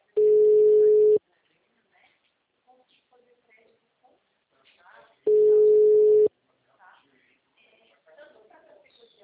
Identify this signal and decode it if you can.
Ringback tone (Europe, 425 Hz), with speech.
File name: out- -2026- - - .79903